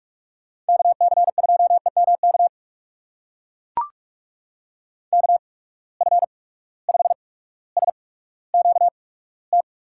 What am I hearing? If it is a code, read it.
Morse code, heard as a keyed tone, about 42 wpm, spaced as for 19 wpm, 700 Hz. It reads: KC2EMK KF5SQT
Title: USA Call Signs - 6 Characters - 8x Character Spacing - 45wpm